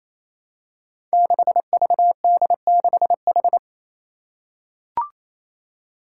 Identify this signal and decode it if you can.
Morse code, heard as a keyed tone, about 29 wpm, 700 Hz. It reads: 6VD6H
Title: Five Challenging Letter-Number Combinations 28wpm